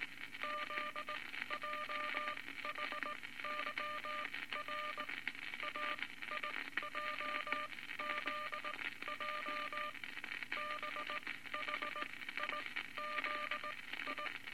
Morse code, heard as a keyed tone, about 19 wpm, spaced as for 14 wpm, 1200 Hz, with noise. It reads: ZJHYR AIJZJ BHIZI
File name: Morse Code Records / 19 B LCM 167 Twelve Words Per Minute